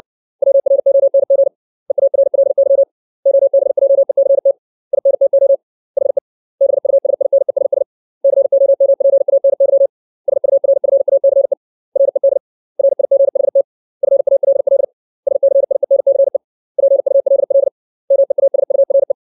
Morse code, heard as a keyed tone, about 60 wpm, 550 Hz. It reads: WRONG EARLY OBJECT INTO HE BUSINESS COMPANY SURFACE RED RIGHT FALL SCIENCE PULL MEASURE
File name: 632 words / 632 Words 60WPM